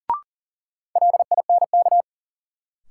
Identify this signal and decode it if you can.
Morse code, heard as a keyed tone, about 40 wpm, 700 Hz. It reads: LINK
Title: Sentences from Top 1000 Words - Review Part 7 - New Word Drill - 40wpm